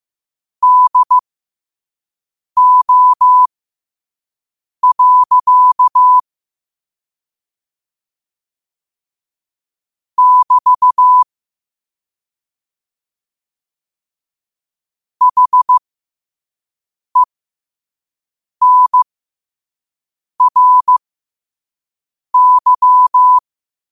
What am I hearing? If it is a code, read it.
Morse code, heard as a keyed tone, about 15 wpm, spaced as for 5 wpm, 1000 Hz. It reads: DO. <BT> HENRY
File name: qotdfiles / 5wpm